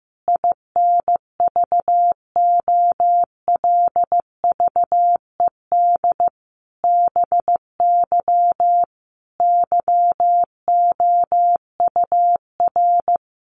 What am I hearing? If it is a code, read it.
Morse code, heard as a keyed tone, about 15 wpm, 700 Hz. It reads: INVOLVED BY YOUR
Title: Sentences from Top 800 Words 15wpm